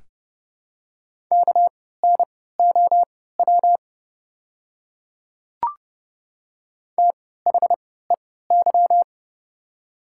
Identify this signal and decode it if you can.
Morse code, heard as a keyed tone, about 30 wpm, spaced as for 17 wpm, 700 Hz. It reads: KNOW THEY
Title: 3x Character Spacing - Top 100 Words - 30wpm